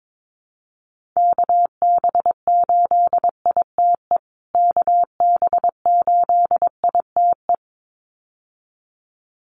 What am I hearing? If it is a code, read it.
Morse code, heard as a keyed tone, about 22 wpm, 700 Hz. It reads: KB8ITE KB8ITE
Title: USA Call Signs Repeated 2x 22wpm